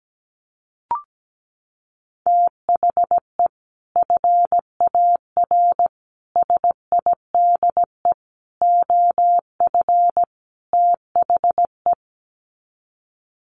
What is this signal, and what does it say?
Morse code, heard as a keyed tone, about 17 wpm, 700 Hz. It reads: THE FAR SIDE OF THE
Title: Sets of 5 Words 17wpm